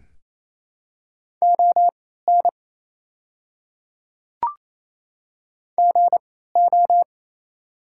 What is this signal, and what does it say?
Morse code, heard as a keyed tone, about 27 wpm, spaced as for 16 wpm, 700 Hz. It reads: ON GO
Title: 3x Character Spacing - Top 300 Words - 28wpm